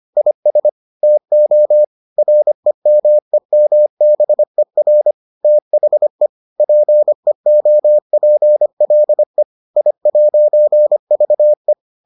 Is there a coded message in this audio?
Yes: Morse code from 0.2 to 11.7 s, heard as a keyed tone, about 25 wpm, 600 Hz. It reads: IS TO REMEMBER THE PEOPLE I'VE